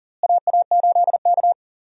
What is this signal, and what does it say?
Morse code, heard as a keyed tone, about 40 wpm, 700 Hz. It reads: AA8K
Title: USA Call Signs - 4 Characters - Repeated 3x - 40wpm